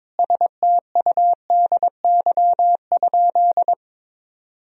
Morse code, heard as a keyed tone, about 21 wpm, 700 Hz. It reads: STUDY?
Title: Sentences from Top 1000 Words - Review Part 2 - 22wpm